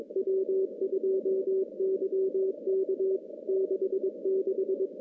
Morse code, heard as a keyed tone, about 22 wpm, 380 Hz. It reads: W2YK 66
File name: contests / CW-SS-k5tr- -0600